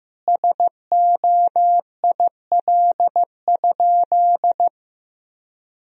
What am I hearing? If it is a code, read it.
Morse code, heard as a keyed tone, about 15 wpm, 700 Hz. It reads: SOIL?
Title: Sentences from Top 1000 Words - Review Part 4 - 15wpm